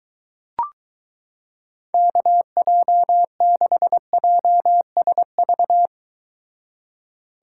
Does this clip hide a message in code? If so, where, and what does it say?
Yes: Morse code from 1.9 to 5.9 s, heard as a keyed tone, about 23 wpm, 700 Hz. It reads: KJ6JSV